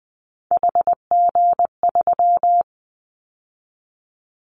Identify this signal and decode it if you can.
Morse code, heard as a keyed tone, about 20 wpm, 700 Hz. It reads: HG3
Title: Callsign Prefixes for Top 50 DXCC - 20wpm